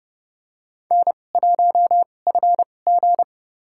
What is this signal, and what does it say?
Morse code, heard as a keyed tone, about 31 wpm, spaced as for 22 wpm, 700 Hz. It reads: N1FG
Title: USA Call Signs - 4 Characters - 2x Character Spacing - 30wpm